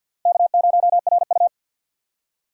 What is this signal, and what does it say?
Morse code, heard as a keyed tone, about 48 wpm, 700 Hz. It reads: K0RU
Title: USA Call Signs - 4x Character Spacing - 50wpm